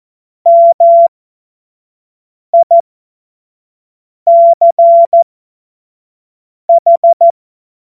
Morse code, heard as a keyed tone, about 14 wpm, spaced as for 5 wpm, 680 Hz. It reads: MICH
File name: Morsekurs / Witze / Witz 17. August 2021, 25 F70